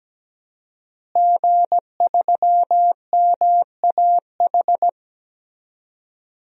Morse code, heard as a keyed tone, about 17 wpm, 700 Hz. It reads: G3MAH
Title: DXCC-100 Call Signs - 5 Characters - Repeated 2x - 17wpm